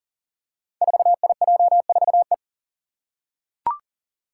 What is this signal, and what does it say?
Morse code, heard as a keyed tone, about 38 wpm, 700 Hz. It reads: 4IJ4E
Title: Five Challenging Letter-Number Combinations 40wpm